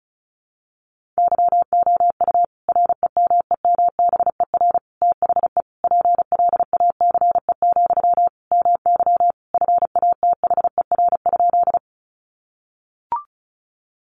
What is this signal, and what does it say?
Morse code, heard as a keyed tone, about 34 wpm, 700 Hz. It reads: YOU REMEMBER THE PLACE, MY FATHER?
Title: Sentences from Top 400 Words 35wpm